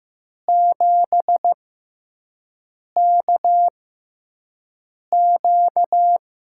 Morse code, heard as a keyed tone, about 15 wpm, spaced as for 5 wpm, 700 Hz. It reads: 7KQ
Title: USA Call Signs - 6x Character Spacing - 15wpm